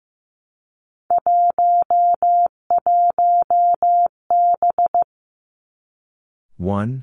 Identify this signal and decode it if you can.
Morse code, heard as a keyed tone, about 15 wpm, 700 Hz. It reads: 11B
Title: Three Challenging Letter-Number Combinations 15wpm